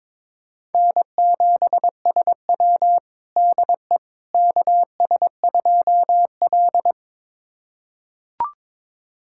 Morse code, heard as a keyed tone, about 21 wpm, 700 Hz. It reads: N7SW DE KS2L
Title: USA Callsign DE Callsign - 4 Characters - 22wpm